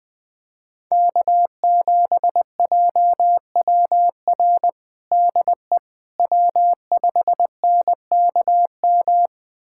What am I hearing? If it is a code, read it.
Morse code, heard as a keyed tone, about 21 wpm, 700 Hz. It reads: K7JWR DE W5NKM
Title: USA Callsign DE Callsign 20wpm